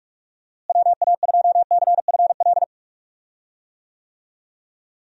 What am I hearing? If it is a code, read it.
Morse code, heard as a keyed tone, about 46 wpm, 700 Hz. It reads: WA2CFL
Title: USA Call Signs - 6 Characters - Repeated 3x - 45wpm